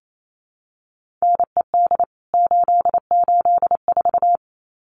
Morse code, heard as a keyed tone, about 28 wpm, 700 Hz. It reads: NED 884
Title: CWT Exchange 28wpm